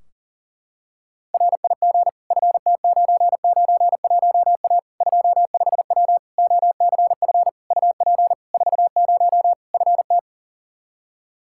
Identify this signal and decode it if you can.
Morse code, heard as a keyed tone, about 40 wpm, 700 Hz. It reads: RIG FT991A 25W OCF UP 40 FT
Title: Practice QSO - Exchange 2 - Info - 40wpm